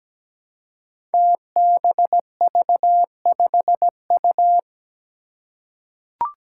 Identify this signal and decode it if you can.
Morse code, heard as a keyed tone, about 17 wpm, 700 Hz. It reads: TBV5U
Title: Five Challenging Letter-Number Combinations 17wpm